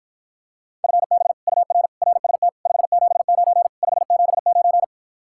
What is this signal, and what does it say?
Morse code, heard as a keyed tone, about 54 wpm, 700 Hz. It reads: FB UR RST 579 579